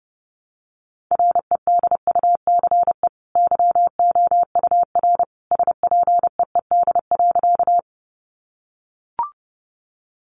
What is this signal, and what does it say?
Morse code, heard as a keyed tone, about 30 wpm, 700 Hz. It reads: REDUCE YOUR SPEED.
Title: Sentences from Top 1100 Words 30wpm